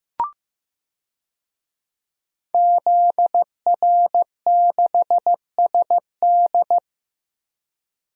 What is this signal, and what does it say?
Morse code, heard as a keyed tone, about 15 wpm, 700 Hz. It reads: ZR6SD
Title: DXCC-100 Call Signs - 5 Characters - Repeated 3x - 15wpm